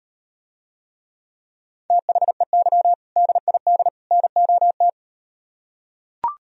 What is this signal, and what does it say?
Morse code, heard as a keyed tone, about 38 wpm, 700 Hz. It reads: THEY DID NOT